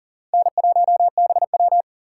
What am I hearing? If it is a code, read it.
Morse code, heard as a keyed tone, about 40 wpm, 700 Hz. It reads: N1BW
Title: USA Call Signs - 4 Characters - 4x Character Spacing - 40wpm